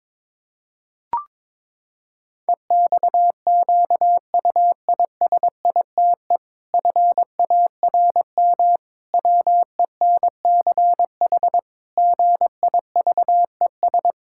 Morse code, heard as a keyed tone, about 22 wpm, 700 Hz. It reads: EXQUISITE FARM WENCH GIVES
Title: Pangram Challenge v2 - 22wpm